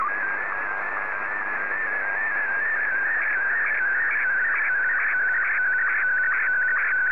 SSTV (slow-scan television), Martin 1.